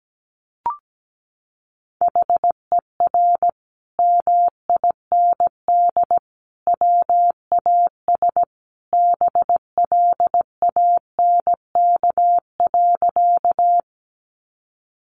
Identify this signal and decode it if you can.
Morse code, heard as a keyed tone, about 17 wpm, 700 Hz. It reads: HER MIND WAS BLANK.
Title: Sentences from Top 1500 Words 17wpm